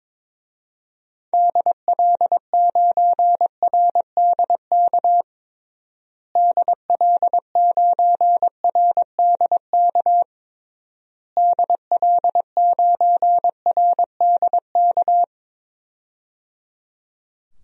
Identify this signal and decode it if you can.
Morse code, heard as a keyed tone, about 22 wpm, 700 Hz. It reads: DL9RDK DL9RDK DL9RDK
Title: DXCC-100 Call Signs - 6 Characters - Repeated 3x - 22wpm